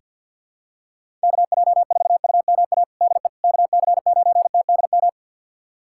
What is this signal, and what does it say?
Morse code, heard as a keyed tone, about 48 wpm, 700 Hz. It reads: KJ4UMA DE KC9TDM
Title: USA Callsign DE Callsign - 6 Characters - 50wpm